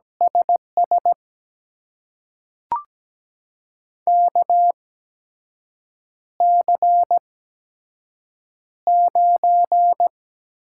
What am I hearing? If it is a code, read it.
Morse code, heard as a keyed tone, about 17 wpm, spaced as for 6 wpm, 700 Hz. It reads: SS KC9
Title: USA Call Signs - 6 Characters - 8x Character Spacing - 17wpm